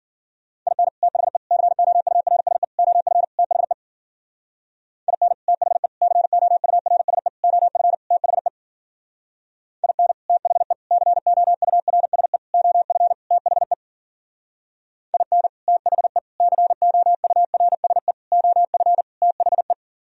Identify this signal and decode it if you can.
Morse code, heard as a keyed tone, about 51 wpm, 700 Hz. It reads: IN THE COURSE OF THE IN THE COURSE OF THE IN THE COURSE OF THE IN THE COURSE OF THE